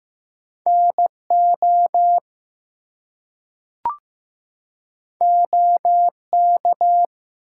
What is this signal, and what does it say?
Morse code, heard as a keyed tone, about 15 wpm, 700 Hz. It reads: NO OK
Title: Two Letter Words - 15wpm